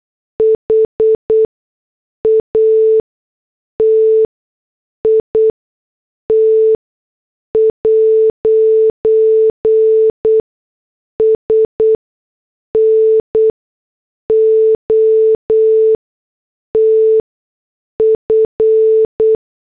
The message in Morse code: HAT IT'S NOT F